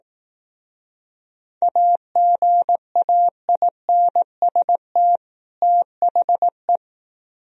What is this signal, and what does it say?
Morse code, heard as a keyed tone, about 18 wpm, 700 Hz. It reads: AGAINST THE
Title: Speed-Racing - Sets of 2 Words 15wpm